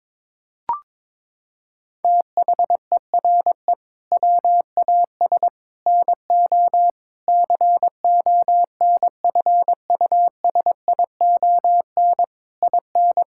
Morse code, heard as a keyed tone, about 22 wpm, 700 Hz. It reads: THERE WAS NO CONFUSION IN
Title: Sentences from Top 1600 Words 22wpm